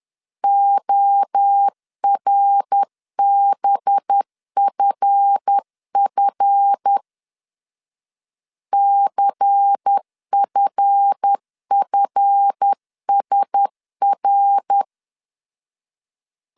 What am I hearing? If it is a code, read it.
Morse code, heard as a keyed tone, about 10 wpm, 800 Hz. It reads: ORBFF CFFSR
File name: LEZIONI / 7 GF9Z / 7 GF 10wpm 2514